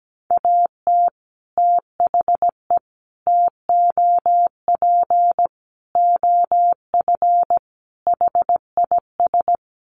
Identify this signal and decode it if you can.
Morse code, heard as a keyed tone, about 17 wpm, 700 Hz. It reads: AT THE TOP OF HIS
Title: Sets of 5 Words Encore 17wpm